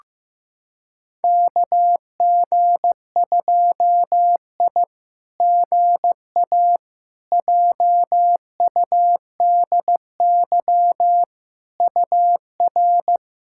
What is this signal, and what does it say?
Morse code, heard as a keyed tone, about 16 wpm, 700 Hz. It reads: KG2I GA JUDY UR
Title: SOTA QSO Practice 15wpm